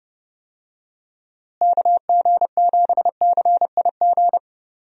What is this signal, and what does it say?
Morse code, heard as a keyed tone, about 32 wpm, 700 Hz. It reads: KG7CIG
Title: USA Call Signs - 6 Characters - Repeated 2x - 30wpm